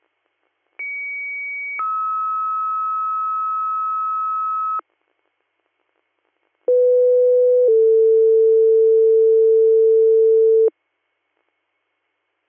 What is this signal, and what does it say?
Two-tone page, heard as a one-second tone, then a three-second tone, about 2250 then 1290 Hz, about 500 then 450 Hz. It is heard from 0.8 to 10.7 s.